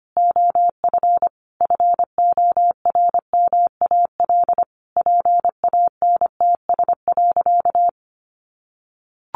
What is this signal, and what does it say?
Morse code, heard as a keyed tone, about 25 wpm, 700 Hz. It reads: OF FORMAL PANTS.
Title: Additional Easy Phrases 25wpm